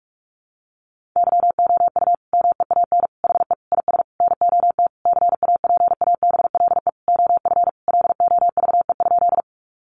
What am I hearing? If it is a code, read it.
Morse code, heard as a keyed tone, about 44 wpm, 700 Hz. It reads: YOU MEAN HE IS NOT CAPABLE OF LOVE?
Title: Sentences from Top 1200 Words 45wpm